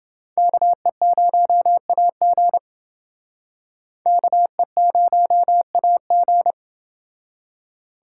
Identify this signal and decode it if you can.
Morse code, heard as a keyed tone, about 29 wpm, 700 Hz. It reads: KE0AG KE0AG